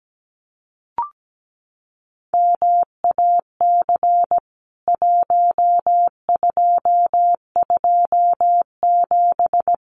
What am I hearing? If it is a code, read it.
Morse code, heard as a keyed tone, about 17 wpm, 700 Hz. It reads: MAC 1227
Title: CWT Exchange 17wpm